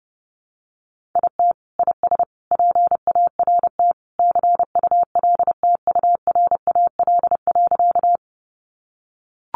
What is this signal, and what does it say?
Morse code, heard as a keyed tone, about 30 wpm, 700 Hz. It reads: IT IS PART CULTURAL.
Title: Sentences from Top 1500 Words 30wpm